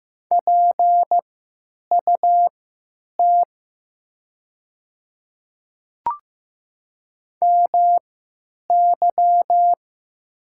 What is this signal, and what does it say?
Morse code, heard as a keyed tone, about 15 wpm, spaced as for 9 wpm, 700 Hz. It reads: PUT MY
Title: 3x Character Spacing - Top 200 Words - 15wpm